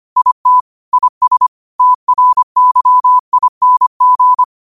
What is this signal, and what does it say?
Morse code, heard as a keyed tone, about 25 wpm, 1000 Hz. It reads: IT IS TRYING
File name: qotdfiles / 25wpm